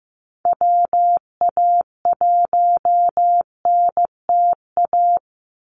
Morse code, heard as a keyed tone, about 15 wpm, 700 Hz. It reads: WA1NTA
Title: Call Signs Repeated 2x - 15wpm